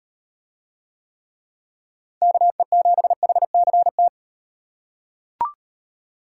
Morse code, heard as a keyed tone, about 35 wpm, 700 Hz. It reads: KE7HCT